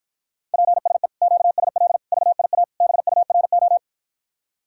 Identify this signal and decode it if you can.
Morse code, heard as a keyed tone, about 53 wpm, 700 Hz. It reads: PSE QSL VIA BURO